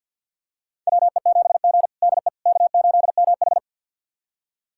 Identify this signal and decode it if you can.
Morse code, heard as a keyed tone, about 48 wpm, 700 Hz. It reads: WE7G DE K8MH
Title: USA Callsign DE Callsign - 4 Characters - 50wpm